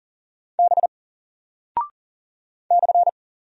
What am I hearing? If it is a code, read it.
Morse code, heard as a keyed tone, about 40 wpm, 700 Hz. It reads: B /
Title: Single Character (Letter:Number:Punctuation:Prosign) Alt 40wpm